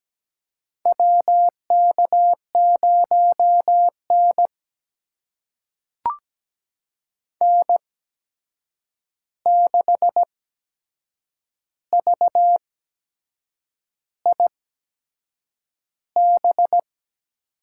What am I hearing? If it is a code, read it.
Morse code, heard as a keyed tone, about 17 wpm, spaced as for 7 wpm, 700 Hz. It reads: WK0N N6VIB
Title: USA Call Signs - 8x Character Spacing - 17wpm